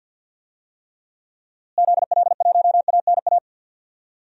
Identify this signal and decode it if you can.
Morse code, heard as a keyed tone, about 50 wpm, 700 Hz. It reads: ZL1ANA
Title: DXCC-100 Call Signs - 6 Characters - Repeated 3x - 50wpm